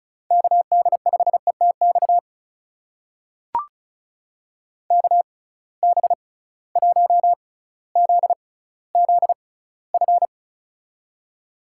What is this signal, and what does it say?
Morse code, heard as a keyed tone, about 36 wpm, spaced as for 18 wpm, 700 Hz. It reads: KD5ETX KB1ZZF
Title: USA Call Signs - 6 Characters - 6x Character Spacing - 35wpm